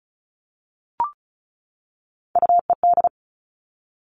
Morse code, heard as a keyed tone, about 36 wpm, 700 Hz. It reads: UED
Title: Three Challenging Letter-Number Combinations 35wpm